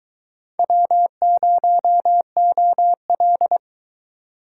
Morse code, heard as a keyed tone, about 22 wpm, 700 Hz. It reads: W0OL